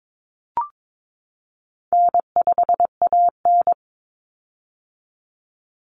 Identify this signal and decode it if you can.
Morse code, heard as a keyed tone, about 22 wpm, 700 Hz. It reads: N5AN
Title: Easy Call Signs - 22wpm